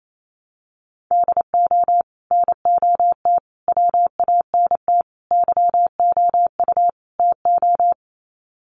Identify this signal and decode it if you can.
Morse code, heard as a keyed tone, about 28 wpm, 700 Hz. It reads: DO NOT WANT YOU TO
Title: Sets of 5 Words 28wpm